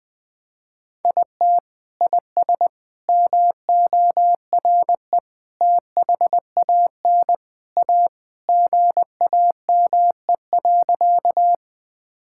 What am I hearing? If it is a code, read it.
Morse code, heard as a keyed tone, about 21 wpm, 700 Hz. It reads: IT IS MORE THAN A GAME.
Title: Sentences with 3-Word Phrases - 20wpm